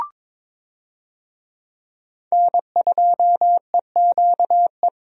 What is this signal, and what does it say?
Morse code, heard as a keyed tone, about 22 wpm, 700 Hz. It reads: N2EQE
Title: USA Call Signs - 5 Characters - Repeated 2x - 22wpm